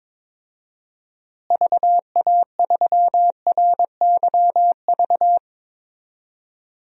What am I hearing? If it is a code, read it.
Morse code, heard as a keyed tone, about 22 wpm, 700 Hz. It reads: VA3RYV
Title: DXCC-100 Call Signs - 6 Characters - Repeated 2x - 22wpm